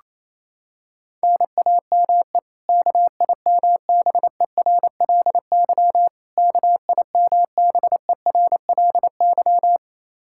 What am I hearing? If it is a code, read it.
Morse code, heard as a keyed tone, about 28 wpm, 700 Hz. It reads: NAME KIMBERLY KIMBERLY
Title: Practice QSO - Exchange 1 - Beginning - 28wpm